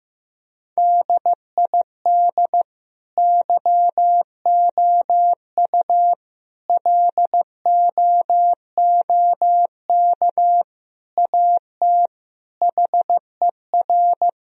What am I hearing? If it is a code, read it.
Morse code, heard as a keyed tone, about 15 wpm, 700 Hz. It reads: DID YOU LOOK AT HER
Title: Sentences from Top 1000 Words - Review Part 4 - 15wpm